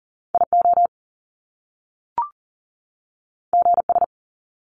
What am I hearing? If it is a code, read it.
Morse code, heard as a keyed tone, about 39 wpm, 700 Hz. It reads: IO GS
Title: Top Two Letter Combinations in Words 40wpm